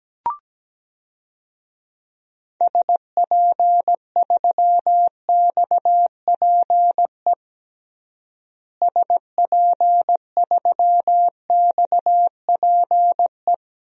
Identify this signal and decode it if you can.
Morse code, heard as a keyed tone, about 17 wpm, 700 Hz. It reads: SP3XPE SP3XPE
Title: DXCC-100 Call Signs - 6 Characters - Repeated 2x - 17wpm